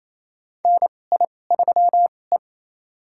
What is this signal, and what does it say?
Morse code, heard as a keyed tone, about 28 wpm, spaced as for 20 wpm, 700 Hz. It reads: NI3E
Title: USA Call Signs - 4 Characters - 2x Character Spacing - 28wpm